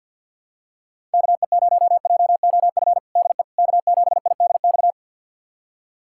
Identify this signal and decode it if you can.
Morse code, heard as a keyed tone, about 48 wpm, 700 Hz. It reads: KE0JOF DE K7IDX
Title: USA Callsign DE Callsign 50wpm